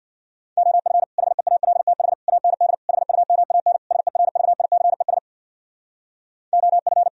Morse code, heard as a keyed tone, about 64 wpm, 700 Hz. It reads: OF HEALTH AND HUMAN SERVICES OF